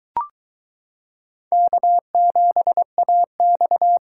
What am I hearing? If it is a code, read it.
Morse code, heard as a keyed tone, about 22 wpm, 700 Hz. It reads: K7AX